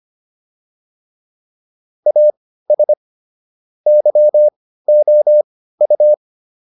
Morse code, heard as a keyed tone, about 25 wpm, spaced as for 15 wpm, 600 Hz. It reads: AS YOU